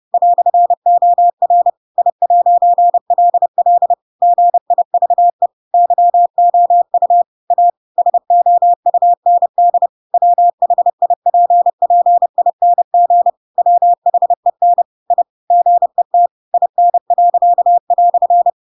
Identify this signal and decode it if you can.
Morse code, heard as a keyed tone, about 30 wpm, 700 Hz. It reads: "OR I'LL GIVE YOU A SOUND WHIPPING WHEN I GET IN."
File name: Pinocchio / Pinocchio ch7 30 30 ik1hge